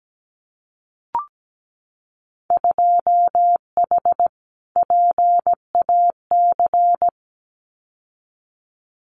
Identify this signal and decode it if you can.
Morse code, heard as a keyed tone, about 17 wpm, 700 Hz. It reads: 2H PAC
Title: Winter Field Day Exchanges Encore 17wpm